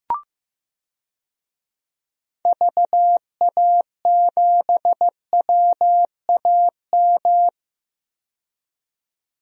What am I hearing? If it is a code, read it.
Morse code, heard as a keyed tone, about 16 wpm, 690 Hz. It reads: VA7WAM
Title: DXCC-100 Call Signs - 6 Characters - 15wpm